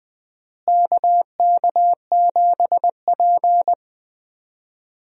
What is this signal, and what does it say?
Morse code, heard as a keyed tone, about 20 wpm, 700 Hz. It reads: KK7P
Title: USA Call Signs - 4 Characters - Repeated 2x - 20wpm